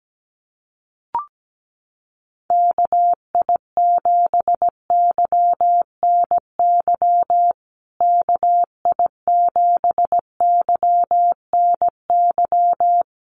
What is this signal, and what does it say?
Morse code, heard as a keyed tone, about 17 wpm, 700 Hz. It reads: KI7YNY KI7YNY
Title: USA Call Signs Repeated 3x 17wpm